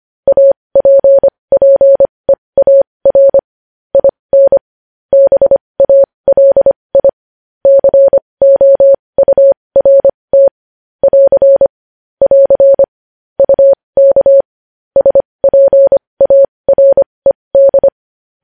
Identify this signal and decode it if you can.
Morse code, heard as a keyed tone, about 25 wpm, spaced as for 20 wpm, 550 Hz. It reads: APPEAR IN BALI COURT <AR> <AR> UK SPARED
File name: podcast / News headlines-1x-25x20